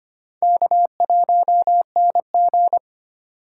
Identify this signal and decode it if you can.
Morse code, heard as a keyed tone, about 25 wpm, 700 Hz. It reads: K1NG
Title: USA Call Signs - 4 Characters - 4x Character Spacing - 25wpm